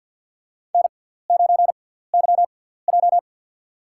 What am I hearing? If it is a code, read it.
Morse code, heard as a keyed tone, about 48 wpm, spaced as for 17 wpm, 700 Hz. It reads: N9YJ
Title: USA Call Signs - 4 Characters - 6x Character Spacing - 50wpm